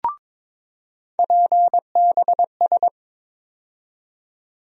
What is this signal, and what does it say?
Morse code, heard as a keyed tone, about 21 wpm, 700 Hz. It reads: PBS
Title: USA Call Sign Postfixes - 3 Letters 22wpm